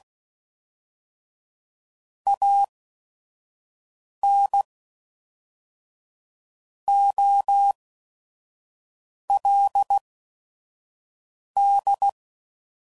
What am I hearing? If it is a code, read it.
Morse code, heard as a keyed tone, about 16 wpm, spaced as for 5 wpm, 800 Hz. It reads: AN OLD
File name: CW Code Groups @5WPM / QSOs / WD0FBT 2